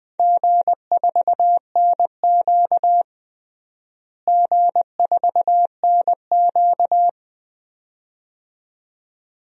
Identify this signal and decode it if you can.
Morse code, heard as a keyed tone, about 20 wpm, 700 Hz. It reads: G4NQ G4NQ
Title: DXCC-100 Call Signs - 4 Characters - Repeated 2x - 20wpm